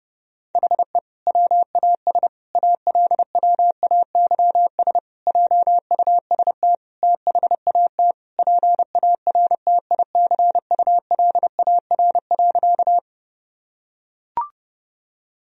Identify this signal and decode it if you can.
Morse code, heard as a keyed tone, about 30 wpm, 700 Hz. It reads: HE WAS ALWAYS JUST THAT PARTICULAR.
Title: Sentences from Top 1000 Words - Review Part 4 - 30wpm